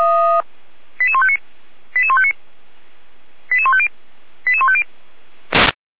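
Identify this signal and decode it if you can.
Five-tone selective call (ZVEI), heard as quick runs of five tones, tones of about 70 ms. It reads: 88159 88159 88150 88150
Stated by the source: ZVEI five-tone call 88271 received by SDR.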